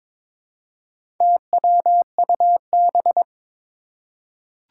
Morse code, heard as a keyed tone, about 22 wpm, 700 Hz. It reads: TWUB
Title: Four Challenging Letter-Number Combinations 22wpm